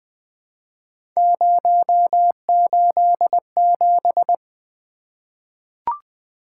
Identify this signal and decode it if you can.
Morse code, heard as a keyed tone, about 20 wpm, 700 Hz. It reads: 087
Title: Three Letter-Number Combinations - 20wpm